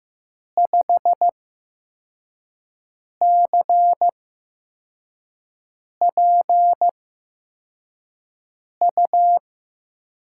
Morse code, heard as a keyed tone, about 15 wpm, spaced as for 4 wpm, 700 Hz. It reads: 5CPU
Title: USA Call Signs - 8x Character Spacing - 15wpm